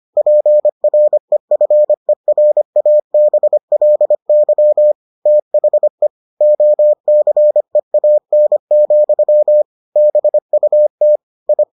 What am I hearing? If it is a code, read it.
Morse code, heard as a keyed tone, about 25 wpm, 600 Hz. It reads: PREFERABLY THE OCEAN, BUT I